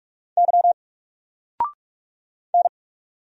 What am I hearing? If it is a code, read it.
Morse code, heard as a keyed tone, about 44 wpm, 700 Hz. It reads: Y N
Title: Single Letter-Number 45wpm